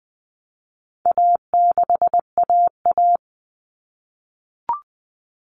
Morse code, heard as a keyed tone, about 20 wpm, 700 Hz. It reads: A6AA
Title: All Cut Numbers - 4-Digit 20wpm